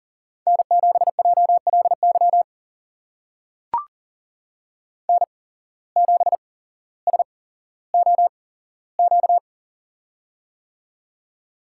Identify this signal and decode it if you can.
Morse code, heard as a keyed tone, about 40 wpm, spaced as for 17 wpm, 700 Hz. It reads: N7JLY N7SOQ
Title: USA Call Signs - 5 Characters - 8x Character Spacing - 40wpm